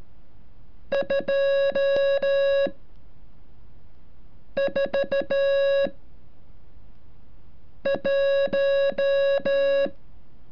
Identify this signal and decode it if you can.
Morse code, heard as a keyed tone, about 14 wpm, 550 Hz. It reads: 2 4 1
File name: curso / ejercicio 33